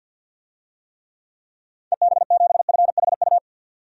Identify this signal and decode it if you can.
Morse code, heard as a keyed tone, about 52 wpm, 700 Hz. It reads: EB7FHU
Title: DXCC-100 Call Signs - 6 Characters - 50wpm